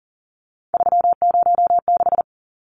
Morse code, heard as a keyed tone, about 40 wpm, 700 Hz. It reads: 306
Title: Numbers - 3-Digit 40wpm